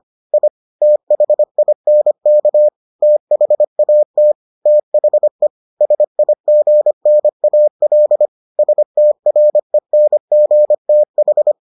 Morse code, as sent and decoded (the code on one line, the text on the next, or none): .. / - .... .. -. -.- / - .... .- - / - .... / ... .. --. -. .- .-.. / ... - .-. -. --. - ....
I THINK THAT THE SIGNAL STRENGTH